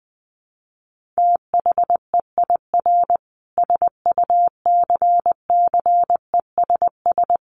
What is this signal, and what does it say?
Morse code, heard as a keyed tone, about 20 wpm, 700 Hz. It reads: THEIR SUCCESS
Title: Sentences from Top 1500 Words 20wpm